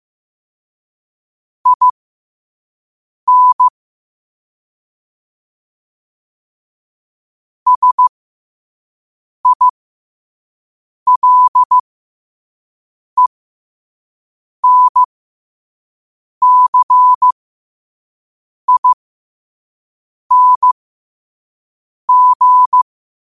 Morse code, heard as a keyed tone, about 15 wpm, spaced as for 5 wpm, 1000 Hz. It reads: IN SILENCING
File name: 5wpm